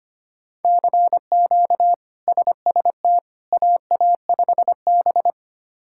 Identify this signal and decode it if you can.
Morse code, heard as a keyed tone, about 25 wpm, 700 Hz. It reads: CQ SST AA5B
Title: K1USN SST QSO Practice - 25wpm